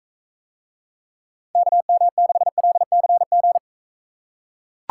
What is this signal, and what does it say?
Morse code, heard as a keyed tone, about 42 wpm, 700 Hz. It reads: KM6LCG